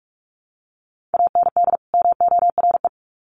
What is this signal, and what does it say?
Morse code, heard as a keyed tone, about 44 wpm, 700 Hz. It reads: AND MORE
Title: Sets of 2 Words 45wpm